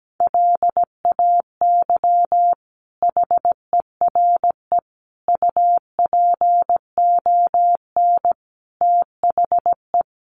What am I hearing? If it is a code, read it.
Morse code, heard as a keyed tone, about 17 wpm, 700 Hz. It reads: LAY HERE UPON THE
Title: Sentences from Top 1100 Words 17wpm